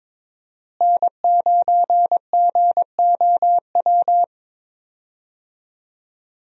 Morse code, heard as a keyed tone, about 22 wpm, 700 Hz. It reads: N9GOW
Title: USA Call Signs - 5 Characters - Repeated 2x - 22wpm